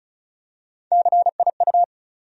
Morse code, heard as a keyed tone, about 34 wpm, 700 Hz. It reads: CIU
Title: USA Call Sign Postfixes - 3 Letters 35wpm